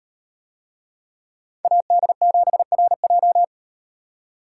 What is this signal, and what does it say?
Morse code, heard as a keyed tone, about 37 wpm, 700 Hz. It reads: AD7RJ